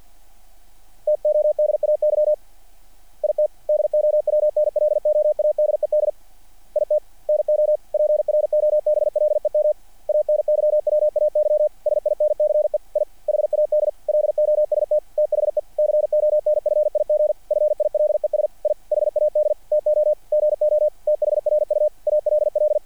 Morse code, heard as a keyed tone, about 48 wpm, 600 Hz. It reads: TODAY IT DOWNLOADED IT NO PROBLEM ANYWAY SINCE I HAD LOST THE CONFIG FILES I HAD TO GO THRU ALL
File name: archived / CRSnet